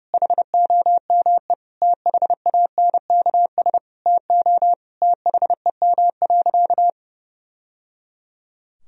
Morse code, heard as a keyed tone, about 30 wpm, 700 Hz. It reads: HOME THANKS TO THEM.
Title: Sentences from Top 1000 Words - Review Part 3 - 30wpm